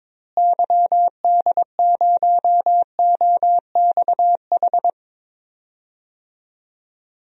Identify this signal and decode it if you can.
Morse code, heard as a keyed tone, about 22 wpm, 700 Hz. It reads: YD0OXH
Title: DXCC-100 Call Signs - 6 Characters - Repeated 2x - 22wpm